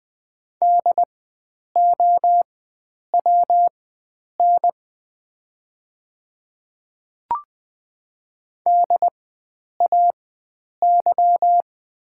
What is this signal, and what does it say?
Morse code, heard as a keyed tone, about 20 wpm, spaced as for 9 wpm, 690 Hz. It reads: DOWN DAY
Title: 4x Character Spacing - Top 100 Words - 20wpm